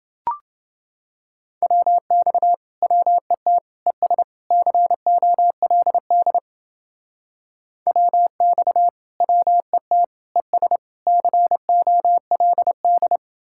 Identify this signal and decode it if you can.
Morse code, heard as a keyed tone, about 29 wpm, 700 Hz. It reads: WX WET ES COLD WX WET ES COLD